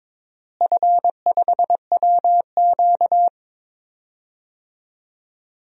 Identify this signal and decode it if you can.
Morse code, heard as a keyed tone, about 22 wpm, 700 Hz. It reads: F5WQ
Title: DXCC-100 Call Signs - 4 Characters - Repeated 2x - 22wpm